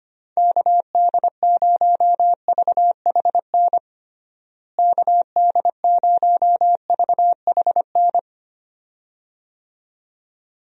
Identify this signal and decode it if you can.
Morse code, heard as a keyed tone, about 25 wpm, 700 Hz. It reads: KD0VHN KD0VHN
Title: USA Call Signs - 6 Characters - Repeated 2x - 25wpm